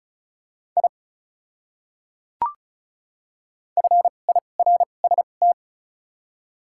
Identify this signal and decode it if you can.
Morse code, heard as a keyed tone, about 36 wpm, spaced as for 25 wpm, 700 Hz. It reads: I FIRST
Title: 2x Character Spacing - Top 100 Words - 35wpm